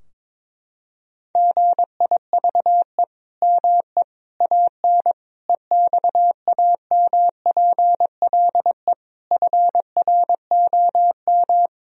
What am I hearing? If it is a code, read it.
Morse code, heard as a keyed tone, about 22 wpm, 700 Hz. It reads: GIVE ME AN EXAMPLE FROM
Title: Sentences from Top 1000 Words - Review Part 3 - 22wpm